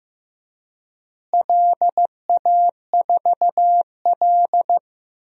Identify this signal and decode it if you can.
Morse code, heard as a keyed tone, about 15 wpm, 700 Hz. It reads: LA4L
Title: DXCC-100 Call Signs - 4 Characters - Repeated 3x - 15wpm